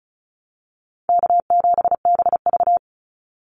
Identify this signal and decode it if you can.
Morse code, heard as a keyed tone, about 36 wpm, 700 Hz. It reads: K7BV
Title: Call Signs - 35wpm